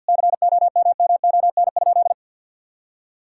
Morse code, heard as a keyed tone, about 50 wpm, 700 Hz. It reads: COMMON?
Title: Sentences from Top 1000 Words - Review Part 3 - 50wpm